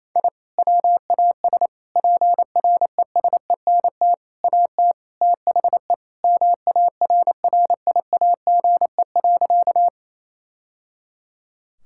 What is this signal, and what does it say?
Morse code, heard as a keyed tone, about 28 wpm, 700 Hz. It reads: I WAS PRESENT AT THE MARRIAGE.
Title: Sentences from Top 1000 Words - Review Part 5 - 28wpm